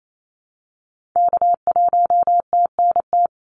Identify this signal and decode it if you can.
Morse code, heard as a keyed tone, about 28 wpm, 700 Hz. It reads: K1TNT
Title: NA SOTA Activator Callsigns 28wpm